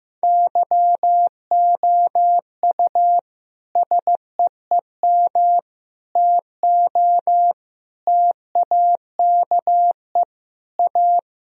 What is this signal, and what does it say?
Morse code, heard as a keyed tone, about 16 wpm, 700 Hz. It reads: YOU SEEM TO TAKE A
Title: Sentences from Top 1000 Words - Review Part 3 - 15wpm